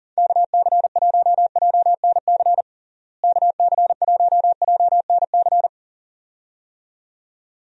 Morse code, heard as a keyed tone, about 40 wpm, 700 Hz. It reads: KC1JNC KC1JNC
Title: USA Call Signs - 6 Characters - Repeated 2x - 40wpm